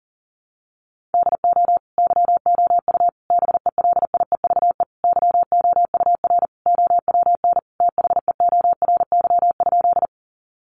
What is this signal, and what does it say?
Morse code, heard as a keyed tone, about 40 wpm, 700 Hz. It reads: DO YOU BELIEVE YOUR OWN THEORY?
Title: Sentences from Top 1000 Words 40wpm